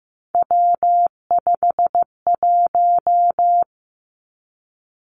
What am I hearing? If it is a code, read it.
Morse code, heard as a keyed tone, about 15 wpm, 700 Hz. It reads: W51
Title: Three Challenging Letter-Number Combinations 15wpm